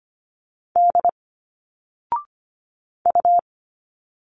Morse code, heard as a keyed tone, about 25 wpm, 700 Hz. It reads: D U
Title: Single Letters 25wpm